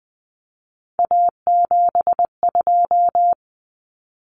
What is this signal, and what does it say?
Morse code, heard as a keyed tone, about 20 wpm, 700 Hz. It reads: A72
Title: Common Cut Numbers - 3-Digit 20wpm